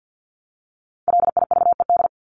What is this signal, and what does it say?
Morse code, heard as a keyed tone, about 50 wpm, 700 Hz. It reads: LIVED
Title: ED Suffix - Top 100 Words with Base Words 50wpm